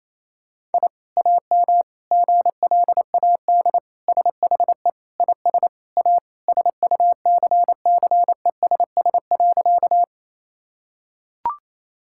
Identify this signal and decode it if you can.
Morse code, heard as a keyed tone, about 28 wpm, 700 Hz. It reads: I AM GLAD SHE IS A SUCCESS.
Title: Sentences from Top 1000 Words - Review Part 6 - 28wpm